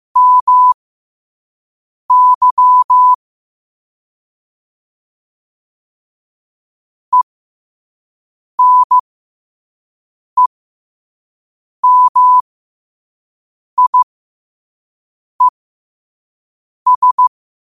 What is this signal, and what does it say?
Morse code, heard as a keyed tone, about 15 wpm, spaced as for 5 wpm, 1000 Hz. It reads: MY ENEMIES
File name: qotdfiles / 5wpm